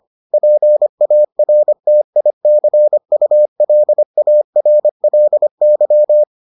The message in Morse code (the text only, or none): PARTICULARLY